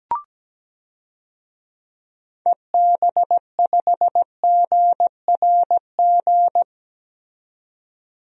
Morse code, heard as a keyed tone, about 17 wpm, 700 Hz. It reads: EB5GRG
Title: DXCC-100 Call Signs - 6 Characters - 17wpm